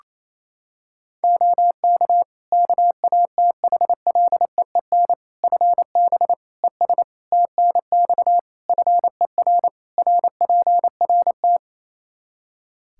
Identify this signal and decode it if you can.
Morse code, heard as a keyed tone, about 28 wpm, 700 Hz. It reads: OK KATHLEEN FB ES TNX FER RPRT
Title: Practice QSO - Exchange 2 - Info - 28wpm